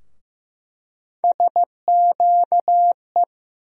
Morse code, heard as a keyed tone, about 15 wpm, 700 Hz. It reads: SQE
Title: USA Call Sign Postfixes - 3 Letters 15wpm